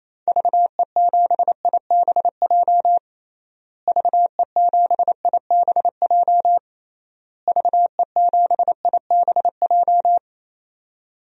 Morse code, heard as a keyed tone, about 28 wpm, 700 Hz. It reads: VE7IBJ VE7IBJ VE7IBJ
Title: DXCC-100 Call Signs - 6 Characters - Repeated 3x - 28wpm